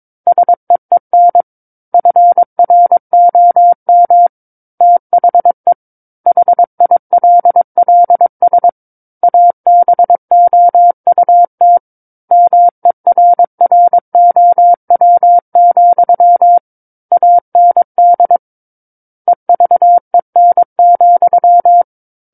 Morse code, heard as a keyed tone, about 22 wpm, 700 Hz. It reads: SEEN FROM THE HILLS ABOUT MERROW, AND EVEN,
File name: Book One / 13-Chapter 13 - 22 WPM